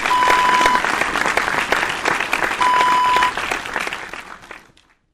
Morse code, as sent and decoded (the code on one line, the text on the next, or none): none